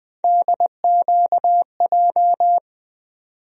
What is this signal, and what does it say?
Morse code, heard as a keyed tone, about 20 wpm, 700 Hz. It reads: DQJ
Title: USA Call Sign Postfixes - 3 Letters 20wpm